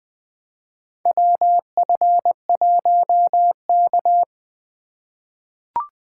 Morse code, heard as a keyed tone, about 21 wpm, 700 Hz. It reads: WF1K
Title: USA Call Signs - 4 Characters - 4x Character Spacing - 20wpm